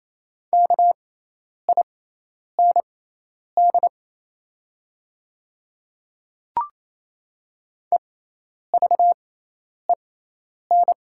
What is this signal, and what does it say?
Morse code, heard as a keyed tone, about 27 wpm, spaced as for 10 wpm, 690 Hz. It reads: KIND EVEN
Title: 6x Character Spacing - Top 300 Words - 28wpm